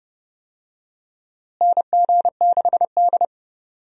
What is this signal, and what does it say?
Morse code, heard as a keyed tone, about 30 wpm, 700 Hz. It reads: NG6D
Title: USA Call Signs - 4 Characters - Repeated 3x - 30wpm